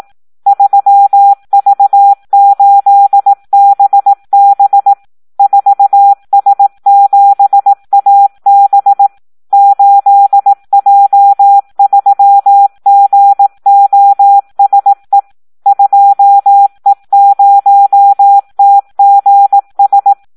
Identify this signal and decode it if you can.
Morse code, heard as a keyed tone, about 18 wpm, 800 Hz. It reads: 3V8BB 4S7AB 8J3GOSE 2E0TGS
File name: Audio practica CW / SD 1 18